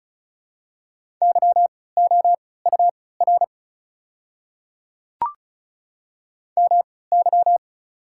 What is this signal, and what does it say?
Morse code, heard as a keyed tone, about 36 wpm, spaced as for 20 wpm, 690 Hz. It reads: YOUR MY
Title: 3x Character Spacing - Top 100 Words - 35wpm